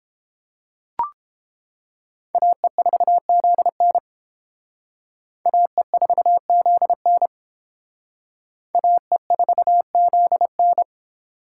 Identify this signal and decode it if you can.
Morse code, heard as a keyed tone, about 28 wpm, 700 Hz. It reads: AE4ZN AE4ZN AE4ZN
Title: Speed Racing - USA 2x2 Call Signs 22wpm